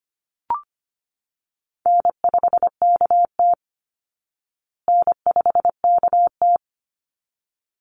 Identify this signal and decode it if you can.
Morse code, heard as a keyed tone, about 25 wpm, 700 Hz. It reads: N5KT N5KT
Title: Easy Call Signs Repeated 2x - 25wpm